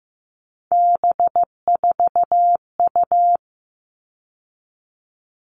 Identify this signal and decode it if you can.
Morse code, heard as a keyed tone, about 15 wpm, 700 Hz. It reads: B4U
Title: All Cut Numbers - 3-Digit 15wpm